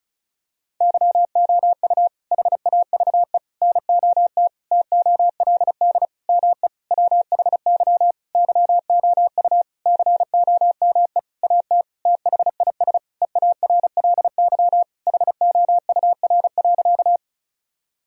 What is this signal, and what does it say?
Morse code, heard as a keyed tone, about 34 wpm, 700 Hz. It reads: YOU HAVE NOT TOLD ME WHY YOU COME AT THIS EARLY HOUR.
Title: Sentences from Top 1000 Words - Review Part 2 - 35wpm